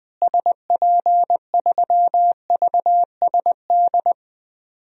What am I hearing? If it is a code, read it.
Morse code, heard as a keyed tone, about 20 wpm, 700 Hz. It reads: SP3VSD
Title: DXCC-100 Call Signs - 6 Characters - Repeated 3x - 20wpm